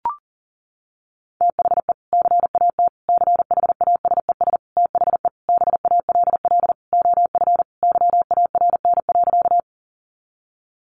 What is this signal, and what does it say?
Morse code, heard as a keyed tone, about 37 wpm, 700 Hz. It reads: THE CAT CHASES THE BALL OF YARN.
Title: Additional Easy Phrases 40wpm